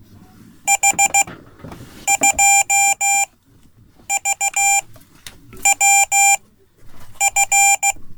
Morse code, heard as a keyed tone, about 16 wpm, spaced as for 8 wpm, 790 Hz. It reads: H2VWF
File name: Code / 5min-c